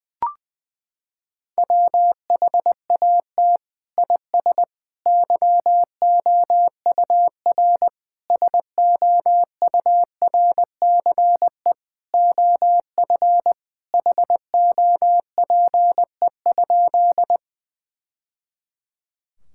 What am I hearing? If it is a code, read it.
Morse code, heard as a keyed tone, about 20 wpm, 700 Hz. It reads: WHAT IS YOUR SOURCE OF HOPE?
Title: Sentences from Top 1000 Words - Review Part 5 - 20wpm